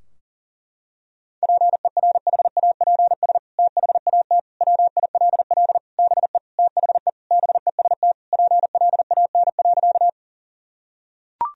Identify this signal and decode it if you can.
Morse code, heard as a keyed tone, about 40 wpm, 700 Hz. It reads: PERHAPS THAT WILL BE THE BEST PLAN.
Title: Sentences from Top 1000 Words - Review Part 2 - 40wpm